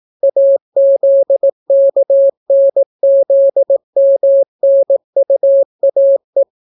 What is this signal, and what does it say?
Morse code, heard as a keyed tone, about 18 wpm, 550 Hz. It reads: AZKNZMNUAE